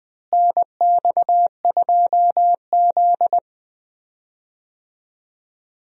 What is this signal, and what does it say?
Morse code, heard as a keyed tone, about 20 wpm, 700 Hz. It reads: NX2Z
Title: USA Call Signs - 4 Characters - Repeated 3x - 20wpm